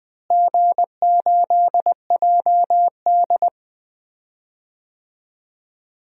Morse code, heard as a keyed tone, about 20 wpm, 700 Hz. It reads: G8JD
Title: DXCC-100 Call Signs - 4 Characters - Repeated 2x - 20wpm